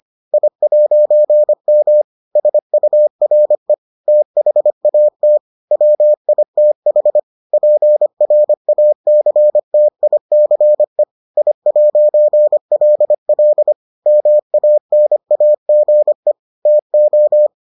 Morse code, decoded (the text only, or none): I'M SURE THAT WITH PRACTICE I'LL MANAGE TO